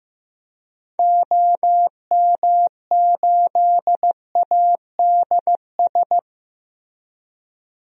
Morse code, heard as a keyed tone, about 15 wpm, 700 Hz. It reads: OM8ADS
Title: DXCC-100 Call Signs - 6 Characters - Repeated 3x - 15wpm